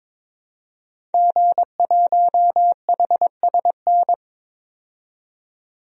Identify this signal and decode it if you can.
Morse code, heard as a keyed tone, about 21 wpm, 700 Hz. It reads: G1HSN
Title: DXCC-100 Call Signs - 5 Characters - 22wpm